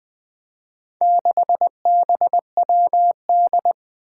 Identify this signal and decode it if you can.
Morse code, heard as a keyed tone, about 20 wpm, 700 Hz. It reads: 6BWD
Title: Four Challenging Letter-Number Combinations 20wpm